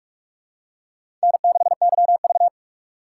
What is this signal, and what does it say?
Morse code, heard as a keyed tone, about 46 wpm, 700 Hz. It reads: N6YV
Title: USA Call Signs - 4 Characters - 3x Character Spacing - 45wpm